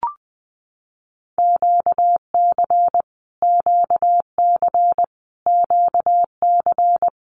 Morse code, heard as a keyed tone, about 20 wpm, 700 Hz. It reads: QC QC QC
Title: Canadian Provinces and Territories - Repeated 3x - 20wpm